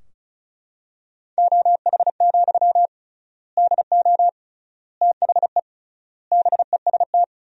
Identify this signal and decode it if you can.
Morse code, heard as a keyed tone, about 34 wpm, 700 Hz. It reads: OH, DO THE BEST
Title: 3x Word-Spacing - Sentences from Top 500 Words - 35wpm